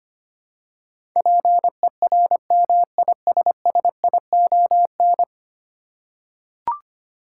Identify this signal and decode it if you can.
Morse code, heard as a keyed tone, about 25 wpm, 700 Hz. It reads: PERMISSION
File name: 10 Letter Words - First 3 Letters Given - 25wpm